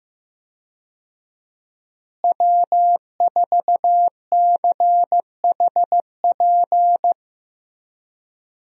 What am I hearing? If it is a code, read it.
Morse code, heard as a keyed tone, about 15 wpm, 700 Hz. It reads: W4CHP
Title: USA Call Signs - 5 Characters - Repeated 2x - 15wpm